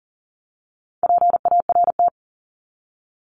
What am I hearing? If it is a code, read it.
Morse code, heard as a keyed tone, about 40 wpm, 700 Hz. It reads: PART
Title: LY Suffix - Top 100 Words with Base Words 40wpm